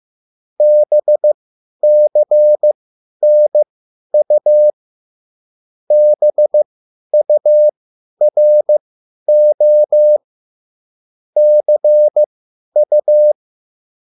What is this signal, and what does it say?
Morse code, heard as a keyed tone, about 15 wpm, spaced as for 10 wpm, 600 Hz. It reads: BCNU BURO CU
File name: QSL / CW lesson